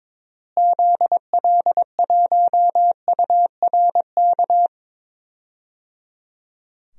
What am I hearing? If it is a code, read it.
Morse code, heard as a keyed tone, about 22 wpm, 700 Hz. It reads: ZL1URK
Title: DXCC-100 Call Signs - 6 Characters - Repeated 3x - 22wpm